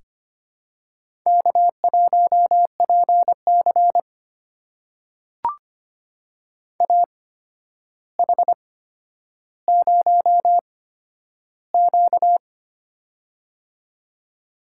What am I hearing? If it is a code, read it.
Morse code, heard as a keyed tone, about 25 wpm, spaced as for 11 wpm, 700 Hz. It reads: K1PC AH0Q
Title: USA Call Signs - 8x Character Spacing - 25wpm